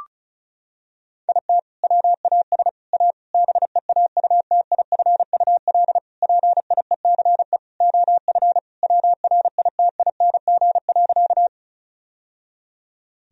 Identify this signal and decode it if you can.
Morse code, heard as a keyed tone, about 34 wpm, 700 Hz. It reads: IT WAS A BEAUTIFUL PIECE OF WRITING.
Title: Sentences from Top 1000 Words - Review Part 2 - 35wpm